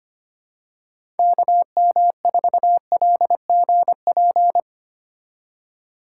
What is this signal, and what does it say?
Morse code, heard as a keyed tone, about 25 wpm, 700 Hz. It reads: KM4LGP
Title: USA Call Signs - 6 Characters - 6x Character Spacing - 25wpm